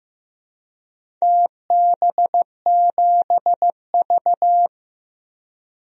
Morse code, heard as a keyed tone, about 15 wpm, 700 Hz. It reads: TB7V
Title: DXCC-100 Call Signs - 4 Characters - Repeated 3x - 15wpm